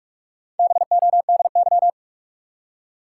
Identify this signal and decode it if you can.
Morse code, heard as a keyed tone, about 46 wpm, 700 Hz. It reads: BODY
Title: Four Letter Words - 45wpm